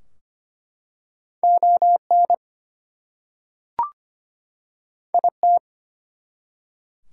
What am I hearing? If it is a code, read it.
Morse code, heard as a keyed tone, about 25 wpm, 700 Hz. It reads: ON IT
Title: Two Letter Words - 25wpm